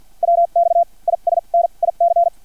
Morse code, heard as a keyed tone, about 50 wpm, spaced as for 39 wpm, 680 Hz. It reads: WX IS NIC
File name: archived / CRSnet